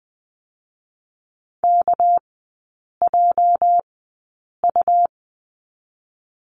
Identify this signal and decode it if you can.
Morse code, heard as a keyed tone, about 20 wpm, spaced as for 8 wpm, 690 Hz. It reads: KJU
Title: 212 Lesson 34e - Number 8 - Three Characters 20wpm